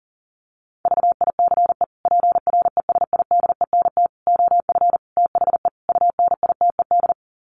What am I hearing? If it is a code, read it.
Morse code, heard as a keyed tone, about 40 wpm, 700 Hz. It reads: VICE PRESIDENT OF THE UNITED
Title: Sets of 5 Words Encore2 40wpm